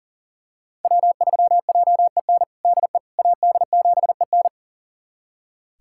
Morse code, heard as a keyed tone, about 40 wpm, 700 Hz. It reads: W3JEN DE AD7EN
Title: USA Callsign DE Callsign - 5 Characters - 40wpm